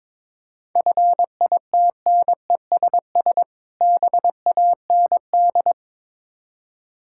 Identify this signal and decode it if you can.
Morse code, heard as a keyed tone, about 23 wpm, 690 Hz. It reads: FITNESS BAND
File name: Difficult Words with Hints - 22wpm